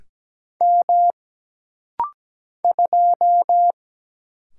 Morse code, heard as a keyed tone, about 17 wpm, 700 Hz. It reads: M 2
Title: Single Letter-Number - Rapid-Fire 17wpm